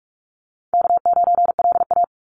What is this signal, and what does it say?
Morse code, heard as a keyed tone, about 44 wpm, 700 Hz. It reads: K9LA
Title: Easy Call Signs Repeated 3x - 45wpm